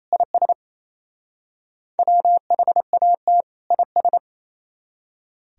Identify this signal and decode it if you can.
Morse code, heard as a keyed tone, about 35 wpm, 700 Hz. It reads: IS WHAT IS
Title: Speed-Racing - Sets of 2 Words 28wpm